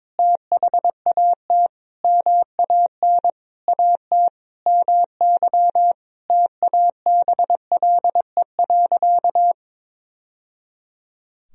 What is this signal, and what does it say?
Morse code, heard as a keyed tone, about 21 wpm, 700 Hz. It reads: THAT MAN AT MY TABLE.
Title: Sentences from Top 1000 Words - Review Part 2 - 22wpm